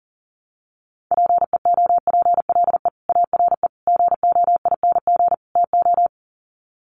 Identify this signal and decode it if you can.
Morse code, heard as a keyed tone, about 40 wpm, 700 Hz. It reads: PEOPLE ARE GOING TO
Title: Sets of 4 Words 40wpm